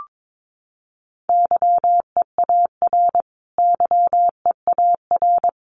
Morse code, heard as a keyed tone, about 22 wpm, 700 Hz. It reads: YEAR YEAR
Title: Top 300 Words Repeated 22wpm